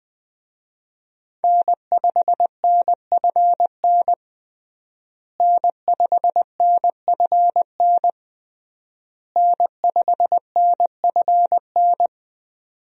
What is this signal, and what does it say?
Morse code, heard as a keyed tone, about 20 wpm, 700 Hz. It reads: N5NFN N5NFN N5NFN
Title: USA Call Signs - 5 Characters - Repeated 3x - 20wpm